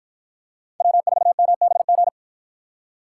Morse code, heard as a keyed tone, about 50 wpm, 700 Hz. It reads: W4MBG